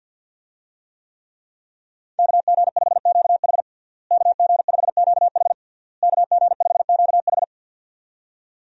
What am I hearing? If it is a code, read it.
Morse code, heard as a keyed tone, about 51 wpm, 700 Hz. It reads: KG5QH KG5QH KG5QH
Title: USA Call Signs - 5 Characters - Repeated 3x - 50wpm